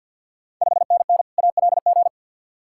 Morse code, heard as a keyed tone, about 53 wpm, 690 Hz. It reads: 5NN ALG
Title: Ontario QSO Party Exchange 50wpm